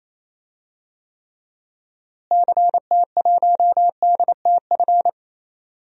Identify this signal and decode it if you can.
Morse code, heard as a keyed tone, about 28 wpm, 700 Hz. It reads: CT1DTF
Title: DXCC-100 Call Signs - 6 Characters - Repeated 2x - 28wpm